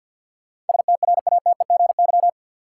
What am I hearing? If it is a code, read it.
Morse code, heard as a keyed tone, about 53 wpm, 700 Hz. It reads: STRATEGY
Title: Sentences from Top 1000 Words - Review Part 5 - New Word Drill - 50wpm